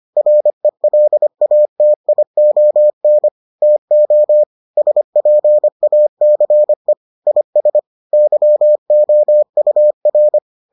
Morse code, heard as a keyed tone, about 25 wpm, 600 Hz. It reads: RELATION TO SPACE IS YOUR